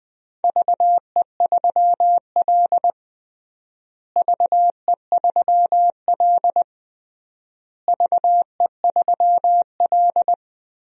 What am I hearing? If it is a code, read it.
Morse code, heard as a keyed tone, about 21 wpm, 700 Hz. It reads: VE3L VE3L VE3L
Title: DXCC-100 Call Signs - 4 Characters - Repeated 3x - 20wpm